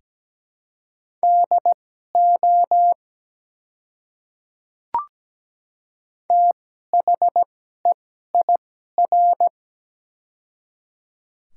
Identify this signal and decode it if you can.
Morse code, heard as a keyed tone, about 17 wpm, spaced as for 12 wpm, 700 Hz. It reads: DO THEIR
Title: 2x Character Spacing - Top 200 Words - 17wpm